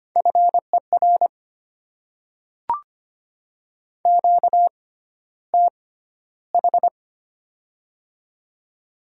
Morse code, heard as a keyed tone, about 25 wpm, spaced as for 13 wpm, 700 Hz. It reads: FER QTH
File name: Top 20 QSO Elements - 6x Character Spacing - 25wpm